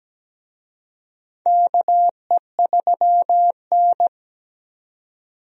Morse code, heard as a keyed tone, about 17 wpm, 700 Hz. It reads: KE3N
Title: USA Call Signs - 4 Characters - 17wpm